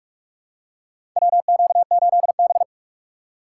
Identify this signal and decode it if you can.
Morse code, heard as a keyed tone, about 46 wpm, 700 Hz. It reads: WQ8B
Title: USA Call Signs - 2x Character Spacing - 45wpm